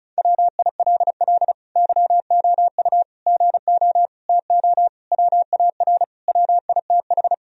Morse code, heard as a keyed tone, about 34 wpm, 700 Hz. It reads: WILL YOU GO TO WAR WITH
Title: Sentences from Top 1000 Words - Review Part 2 - 35wpm